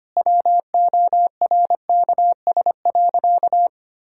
Morse code, heard as a keyed tone, about 25 wpm, 700 Hz. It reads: WORKS.
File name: practice / 25wpm / Sentences with 3-Word Phrases - 25wpm